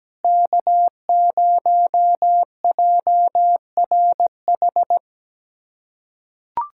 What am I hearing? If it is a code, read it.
Morse code, heard as a keyed tone, about 17 wpm, 700 Hz. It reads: K0JRH
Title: USA Callsign DE Callsign 17wpm